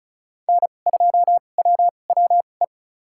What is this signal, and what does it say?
Morse code, heard as a keyed tone, about 36 wpm, spaced as for 25 wpm, 700 Hz. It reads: N2WWE
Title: USA Call Signs - 2x Character Spacing - 35wpm